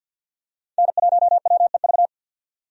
Morse code, heard as a keyed tone, about 50 wpm, 700 Hz. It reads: N1WEV
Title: USA Call Signs - 5 Characters - 50wpm